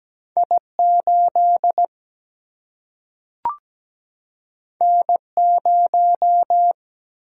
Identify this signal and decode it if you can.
Morse code, heard as a keyed tone, about 17 wpm, 700 Hz. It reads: I8 N0
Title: Single-Letter Single-Number 17wpm